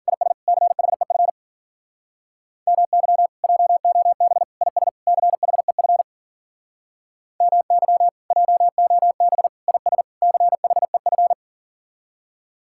Morse code, heard as a keyed tone, about 52 wpm, 700 Hz. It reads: IS CHEF MY JOB IS CHEF MY JOB IS CHEF